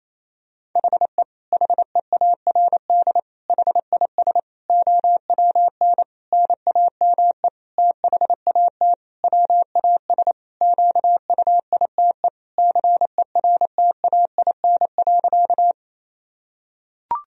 Morse code, heard as a keyed tone, about 28 wpm, 700 Hz. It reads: HE HEARD HIS OWN NAME THAT WAS QUITE CERTAIN.
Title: Sentences from Top 1000 Words - Review Part 2 - 28wpm